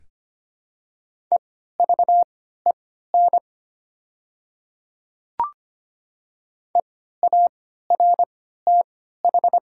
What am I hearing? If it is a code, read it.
Morse code, heard as a keyed tone, about 25 wpm, spaced as for 14 wpm, 690 Hz. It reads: EVEN EARTH
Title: 3x Character Spacing - Top 300 Words - 25wpm